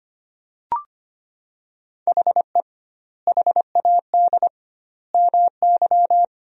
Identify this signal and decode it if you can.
Morse code, heard as a keyed tone, about 25 wpm, 700 Hz. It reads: HE HAD MY
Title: 2x Word-Spacing - Sentences from Top 400 Words - 25wpm